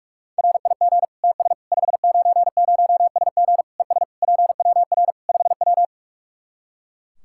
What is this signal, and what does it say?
Morse code, heard as a keyed tone, about 46 wpm, 700 Hz. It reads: RIG TS 590SG ES PWR 5W
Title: Practice QSO - Exchange 2 - Info - 45wpm